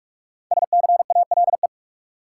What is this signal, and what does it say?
Morse code, heard as a keyed tone, about 46 wpm, 700 Hz. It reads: SCALE
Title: Sentences from Top 1000 Words - Review Part 4 - New Word Drill - 45wpm